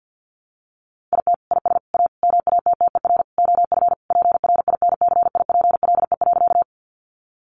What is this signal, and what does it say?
Morse code, heard as a keyed tone, about 50 wpm, 700 Hz. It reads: IT IS A MATTER OF PRINCIPLE.
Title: Sentences from Top 1400 Words 50wpm